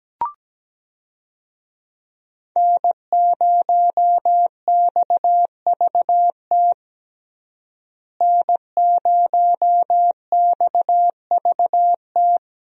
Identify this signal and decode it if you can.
Morse code, heard as a keyed tone, about 17 wpm, 700 Hz. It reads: N0XVT N0XVT
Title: USA Call Signs - 5 Characters - Repeated 2x - 17wpm